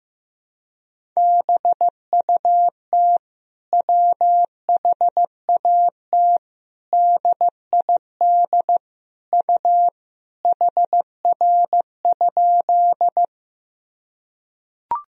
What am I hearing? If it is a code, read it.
Morse code, heard as a keyed tone, about 15 wpm, 700 Hz. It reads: BUT WHAT DID U HR?
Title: Sentences from Top 300 Words and Abbreviations 15wpm